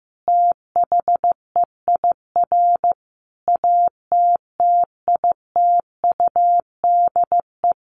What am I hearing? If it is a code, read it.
Morse code, heard as a keyed tone, about 15 wpm, 700 Hz. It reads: THEIR ATTITUDE
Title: Sentences from Top 1300 Words 15wpm